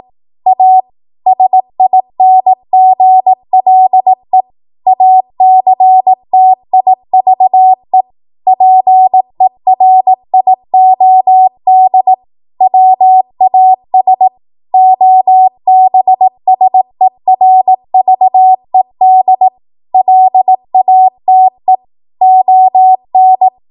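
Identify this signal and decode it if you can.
Morse code, heard as a keyed tone, about 18 wpm, 750 Hz. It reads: A SINGLE ACTIVE PERIOD WAS OBSERVED LATE ON
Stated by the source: ARRL 18 wpm Morse code practice archive transmission.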